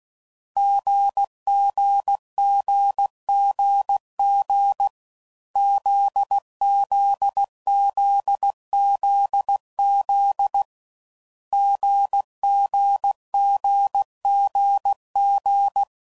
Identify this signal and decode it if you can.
Morse code, heard as a keyed tone, about 16 wpm, 800 Hz. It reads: GGGGG ZZZZZ GGGGG
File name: Combinations / Difficult Combos / GZ GZ7